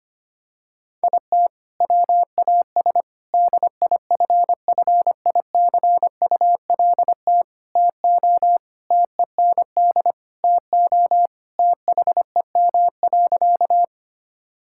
Morse code, heard as a keyed tone, about 25 wpm, 700 Hz. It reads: IT WAS DIFFICULT TO TEND TO THEM.
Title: Sentences from Top 1000 Words - Review Part 5 - 25wpm